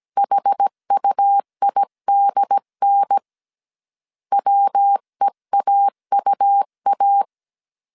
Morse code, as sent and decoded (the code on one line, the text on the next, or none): .... ..- .. -.. -. / .-- . .- ..- .-
HUIDN WEAUA